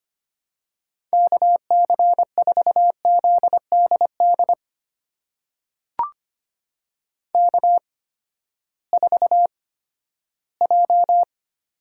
Morse code, heard as a keyed tone, about 25 wpm, spaced as for 14 wpm, 700 Hz. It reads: KC4ZDD K4J
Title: USA Call Signs - 8x Character Spacing - 25wpm